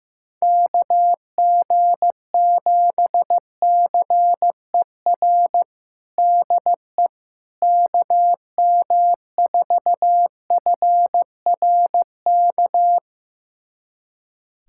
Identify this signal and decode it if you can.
Morse code, heard as a keyed tone, about 15 wpm, 700 Hz. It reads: KG7CER DE KM4FRK
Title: USA Callsign DE Callsign 15wpm